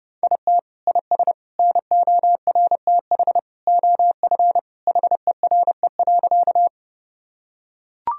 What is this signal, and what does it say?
Morse code, heard as a keyed tone, about 30 wpm, 700 Hz. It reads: IT IS NORTH OF HERE.
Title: Sentences from Top 1000 Words - Review Part 1 - 30wpm